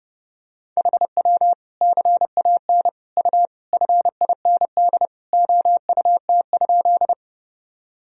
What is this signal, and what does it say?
Morse code, heard as a keyed tone, about 32 wpm, 700 Hz. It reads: HW CAN U FIND OUT?
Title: Sentences from Top 200 Words and Abbreviations 30wpm